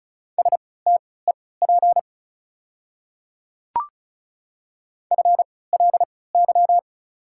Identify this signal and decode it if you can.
Morse code, heard as a keyed tone, about 36 wpm, spaced as for 20 wpm, 700 Hz. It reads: STEP FLY
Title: 3x Character Spacing - Top 500 Words - 35wpm